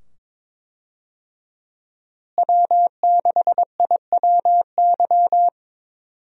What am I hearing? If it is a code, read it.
Morse code, heard as a keyed tone, about 22 wpm, 700 Hz. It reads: W6IWY
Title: USA Call Signs - 5 Characters - Repeated 2x - 22wpm